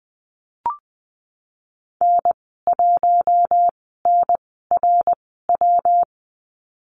Morse code, heard as a keyed tone, about 19 wpm, spaced as for 14 wpm, 700 Hz. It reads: N1NRW
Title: 128 Lesson 21g - Number 6 - Call Signs 20wpm